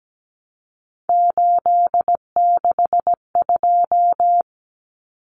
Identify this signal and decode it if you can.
Morse code, heard as a keyed tone, about 17 wpm, 700 Hz. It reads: 862
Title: Numbers - 3-Digit 17wpm